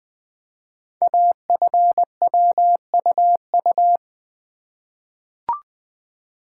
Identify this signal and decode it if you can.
Morse code, heard as a keyed tone, about 20 wpm, 700 Hz. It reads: AFWUU
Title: Code Groups - 5 Letters - 20wpm